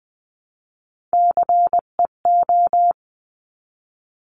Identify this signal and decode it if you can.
Morse code, heard as a keyed tone, about 20 wpm, 700 Hz. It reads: CEO
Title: 259 Lesson 42d - Words 18-20wpm